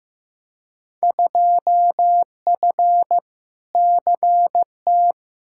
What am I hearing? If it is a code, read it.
Morse code, heard as a keyed tone, about 15 wpm, 700 Hz. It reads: 2F CT
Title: Field Day Exchanges Encore - 15wpm